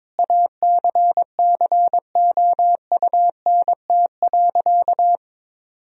Morse code, heard as a keyed tone, about 22 wpm, 700 Hz. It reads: ACCOUNT.
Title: Sentences from Top 1000 Words - Review Part 4 - 22wpm